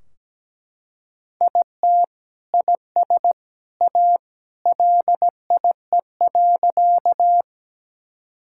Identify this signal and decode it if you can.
Morse code, heard as a keyed tone, about 17 wpm, 700 Hz. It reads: IT IS A LIE.
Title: Sentences from Top 1600 Words 17wpm